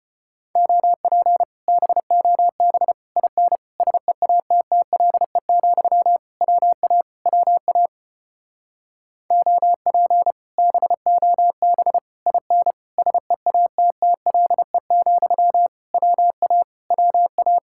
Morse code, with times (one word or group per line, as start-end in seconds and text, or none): 0.6-1.4 OP
1.7-2.9 BOB
3.2-3.6 IN
3.8-6.2 SEATTLE,
6.4-7.0 WA
7.3-7.9 WA
9.3-10.3 OP
10.6-12.0 BOB
12.3-12.7 IN
13.0-15.7 SEATTLE,
15.9-16.6 WA
16.9-17.6 WA